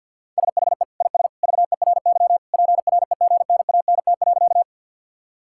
Morse code, heard as a keyed tone, about 50 wpm, 700 Hz. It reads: SHE IS VERY PREGNANT.
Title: Sentences from Top 1000 Words - Review Part 6 - 50wpm